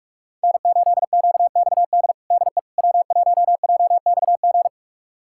Morse code, heard as a keyed tone, about 46 wpm, 700 Hz. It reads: N8QXD DE W1JXG
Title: USA Callsign DE Callsign 45wpm